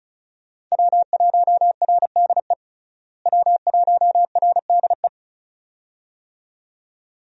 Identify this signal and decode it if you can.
Morse code, heard as a keyed tone, about 34 wpm, 700 Hz. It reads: W1RDE W1RDE
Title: USA Call Signs - 5 Characters - Repeated 2x - 35wpm